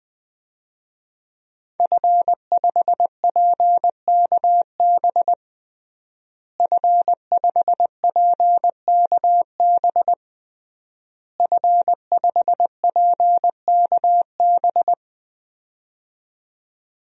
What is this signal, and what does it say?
Morse code, heard as a keyed tone, about 20 wpm, 700 Hz. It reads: F5PKB F5PKB F5PKB
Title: DXCC-100 Call Signs - 5 Characters - Repeated 3x - 20wpm